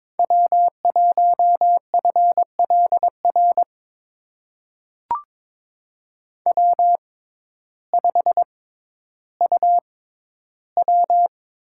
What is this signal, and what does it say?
Morse code, heard as a keyed tone, about 22 wpm, spaced as for 12 wpm, 700 Hz. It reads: W1FLR W5UW
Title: USA Call Signs - 5 Characters - 6x Character Spacing - 22wpm